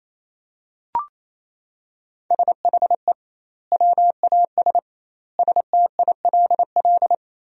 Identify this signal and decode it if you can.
Morse code, heard as a keyed tone, about 28 wpm, 700 Hz. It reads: SHE WAS STILL
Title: 2x Word-Spacing - Sentences from Top 500 Words - 28wpm